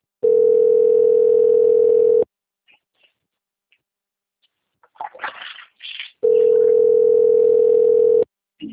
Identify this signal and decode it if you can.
Ringback tone (North America), 440 + 480 Hz.